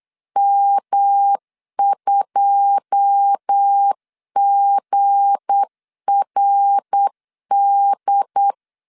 Morse code, heard as a keyed tone, about 8 wpm, 800 Hz. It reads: M2GRD
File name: LEZIONI / 9 2 8wpm 3437